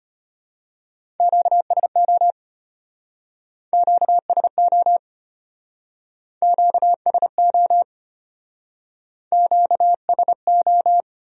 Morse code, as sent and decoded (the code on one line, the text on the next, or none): --.- ... --- / --.- ... --- / --.- ... --- / --.- ... ---
QSO QSO QSO QSO